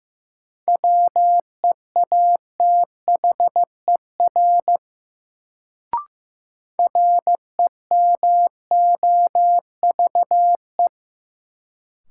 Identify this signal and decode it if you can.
Morse code, heard as a keyed tone, about 15 wpm, 700 Hz. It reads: WEATHER REMOVE
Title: Sentences from Top 800 Words - New Word Drill - 15wpm